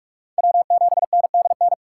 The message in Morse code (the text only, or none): W7NDN